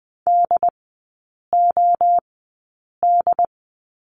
Morse code, heard as a keyed tone, about 20 wpm, spaced as for 8 wpm, 700 Hz. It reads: DOD
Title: 087 Lesson 15e - Number 5 - Three Characters 20wpm